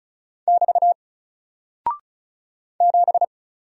Morse code, heard as a keyed tone, about 35 wpm, 700 Hz. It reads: <BT> 7
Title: Single Character (Letter:Number:Punctuation:Prosign) 35wpm